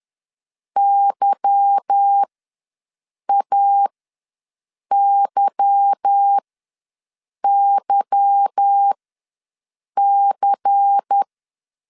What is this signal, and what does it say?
Morse code, heard as a keyed tone, about 10 wpm, spaced as for 6 wpm, 800 Hz. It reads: YAYYC
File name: LEZIONI / 8 YJ45 / 8 Y ws 10wpm 3952